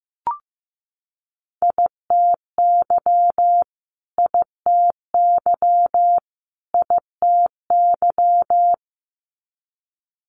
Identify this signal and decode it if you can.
Morse code, heard as a keyed tone, about 15 wpm, 700 Hz. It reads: ITY ITY ITY
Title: Multi-letter Phonemes Repeated 3x 15wpm